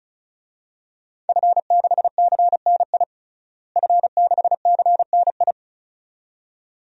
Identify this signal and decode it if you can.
Morse code, heard as a keyed tone, about 36 wpm, 700 Hz. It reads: F6CNI F6CNI
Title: DXCC-100 Call Signs - 5 Characters - Repeated 2x - 35wpm